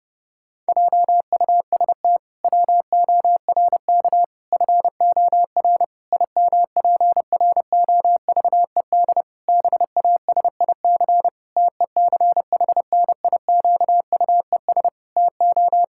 Morse code, heard as a keyed tone, about 32 wpm, 700 Hz. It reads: JUST WORK FOR IMPROVED BASIC TECHNIQUES TO
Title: Pangram Challenge v2 - 30wpm